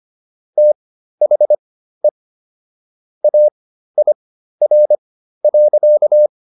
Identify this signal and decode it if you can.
Morse code, heard as a keyed tone, about 25 wpm, spaced as for 13 wpm, 600 Hz. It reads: THE AIR.